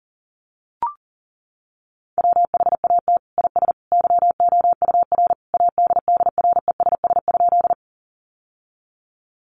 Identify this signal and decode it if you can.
Morse code, heard as a keyed tone, about 40 wpm, 700 Hz. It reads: WHAT IS YOUR ADDRESS?
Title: Sentences from Top 1000 Words 40wpm